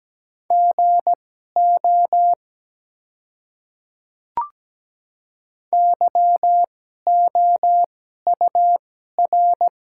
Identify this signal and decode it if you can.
Morse code, heard as a keyed tone, about 17 wpm, spaced as for 12 wpm, 700 Hz. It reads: GO YOUR
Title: 2x Character Spacing - Top 100 Words - 17wpm